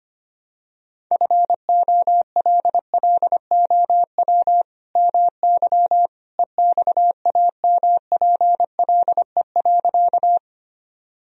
Morse code, heard as a keyed tone, about 25 wpm, 700 Hz. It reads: FOLLOW MY EXAMPLE.
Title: Sentences from Top 1000 Words - Review Part 1 - 25wpm